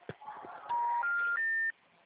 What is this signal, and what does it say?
Special information tone, with noise.